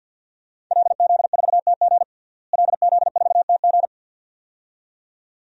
Morse code, heard as a keyed tone, about 48 wpm, 700 Hz. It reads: LZ4TG LZ4TG
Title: DXCC-100 Call Signs - 5 Characters - Repeated 2x - 50wpm